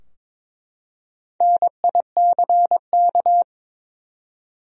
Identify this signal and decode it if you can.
Morse code, heard as a keyed tone, about 22 wpm, 700 Hz. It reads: NICK
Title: USA Common Names - Max 4 Letters - 22wpm